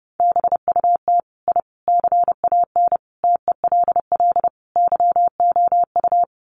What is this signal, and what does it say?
Morse code, heard as a keyed tone, about 30 wpm, 700 Hz. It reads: BUT I CAN TELL YOU
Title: Sets of 5 Words Encore 30wpm